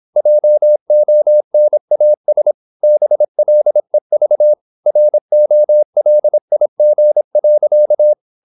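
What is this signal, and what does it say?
Morse code, heard as a keyed tone, about 26 wpm, 600 Hz. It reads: JONAS BLEV ROLIG.
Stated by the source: High-speed (QRQ) Morse at 26 wpm.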